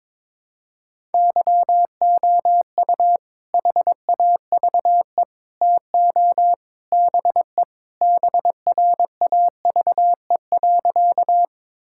Morse code, heard as a keyed tone, about 22 wpm, 700 Hz. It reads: YOU HAVE TO BE BRAVE.
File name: practice / 22wpm / Sentences with 3-Word Phrases - 22wpm